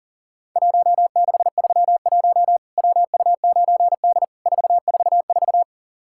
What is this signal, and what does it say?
Morse code, heard as a keyed tone, about 40 wpm, 700 Hz. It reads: 1631 WU9D 444
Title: Worked All Europe DX Contest - QTC Exchange - 40wpm